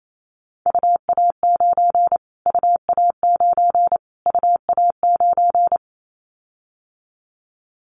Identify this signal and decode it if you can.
Morse code, heard as a keyed tone, about 28 wpm, 700 Hz. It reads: UA9 UA9 UA9
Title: Callsign Prefixes for Top 25 DXCC - Repeated 3x - 28wpm